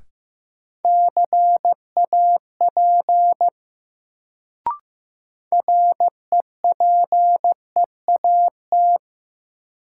Morse code, heard as a keyed tone, about 15 wpm, 700 Hz. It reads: CAP REPEAT
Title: Sentences from Top 1600 Words - New Word Drill Alt - 15wpm